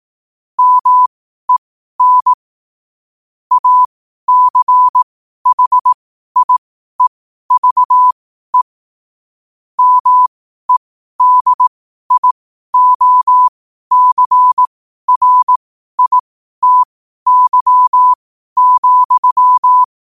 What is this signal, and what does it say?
Morse code, heard as a keyed tone, about 18 wpm, spaced as for 12 wpm, 1000 Hz. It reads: MEN ACHIEVE MEDIOCRITY,